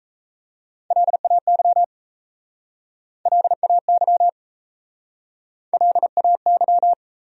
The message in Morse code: LAY LAY LAY

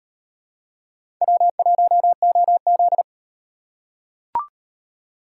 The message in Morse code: W1OZ

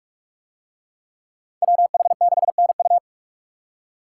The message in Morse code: WH6NU

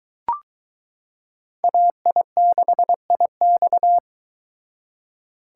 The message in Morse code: AI6IX

